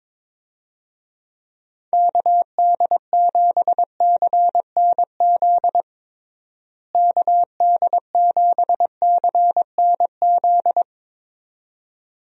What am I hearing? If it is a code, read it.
Morse code, heard as a keyed tone, about 22 wpm, 700 Hz. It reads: KD7CNZ KD7CNZ
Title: USA Call Signs - 6 Characters - Repeated 2x - 22wpm